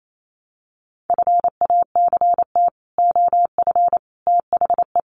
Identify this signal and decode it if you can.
Morse code, heard as a keyed tone, about 28 wpm, 700 Hz. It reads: FACT OF THE
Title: Sets of 5 Words 28wpm